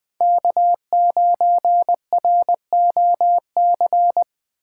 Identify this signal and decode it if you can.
Morse code, heard as a keyed tone, about 21 wpm, 700 Hz. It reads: K9ROC